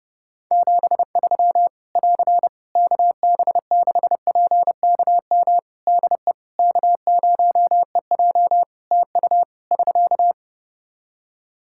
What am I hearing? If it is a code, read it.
Morse code, heard as a keyed tone, about 32 wpm, 700 Hz. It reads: 73 <AR> KB6PKM DE K0EJ TU <SK>
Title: Practice QSO - Exchange 4 - Ending - 30wpm